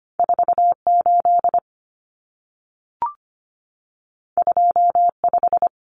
Morse code, heard as a keyed tone, about 25 wpm, 700 Hz. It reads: 48 25
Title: Numbers - 2-Digit 25wpm